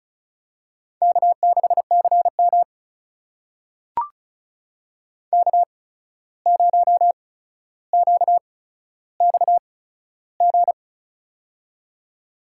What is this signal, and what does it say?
Morse code, heard as a keyed tone, about 36 wpm, spaced as for 14 wpm, 700 Hz. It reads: K6CM K0QXG
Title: USA Call Signs - 8x Character Spacing - 35wpm